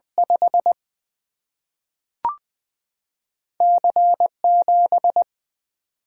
Morse code, heard as a keyed tone, about 20 wpm, 700 Hz. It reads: 5 C 7
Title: Single-Letter Single-Number 20wpm